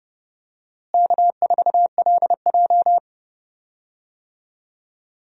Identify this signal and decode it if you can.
Morse code, heard as a keyed tone, about 29 wpm, 700 Hz. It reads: K4LJ
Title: USA Call Signs - 4 Characters - Repeated 3x - 30wpm